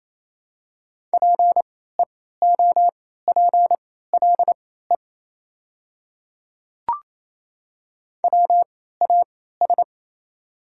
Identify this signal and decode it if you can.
Morse code, heard as a keyed tone, about 28 wpm, spaced as for 16 wpm, 700 Hz. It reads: PEOPLE WAS
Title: 3x Character Spacing - Top 200 Words - 28wpm